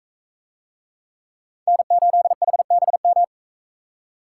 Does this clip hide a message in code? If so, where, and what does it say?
Yes: Morse code from 1.7 to 3.2 s, heard as a keyed tone, about 41 wpm, 700 Hz. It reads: N8HBM